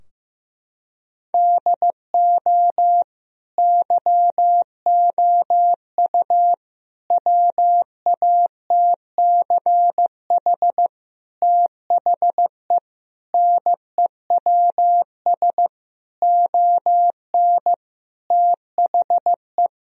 Morse code, heard as a keyed tone, about 15 wpm, 700 Hz. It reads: DO YOU WATCH THE NEWS ON THE
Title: Sentences from Top 1000 Words - Review Part 4 - 15wpm